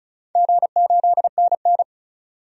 Morse code, heard as a keyed tone, about 36 wpm, 700 Hz. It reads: G8NN
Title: DXCC-100 Call Signs - 4 Characters - 35wpm